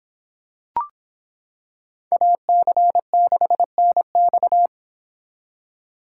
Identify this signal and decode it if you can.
Morse code, heard as a keyed tone, about 25 wpm, 700 Hz. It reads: AC6NX